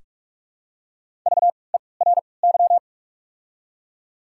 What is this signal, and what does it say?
Morse code, heard as a keyed tone, about 42 wpm, spaced as for 25 wpm, 690 Hz. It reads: VERY
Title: 3x Character Spacing - Top 300 Words - 45wpm